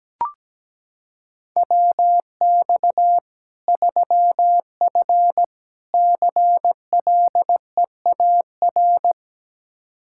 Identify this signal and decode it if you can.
Morse code, heard as a keyed tone, about 17 wpm, 700 Hz. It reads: WX 3F CLEAR
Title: K1USN SST QSO Practice 17wpm